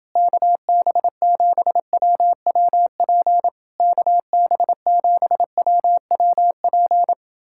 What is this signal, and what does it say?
Morse code, heard as a keyed tone, about 30 wpm, 700 Hz. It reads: KB7WWP KB7WWP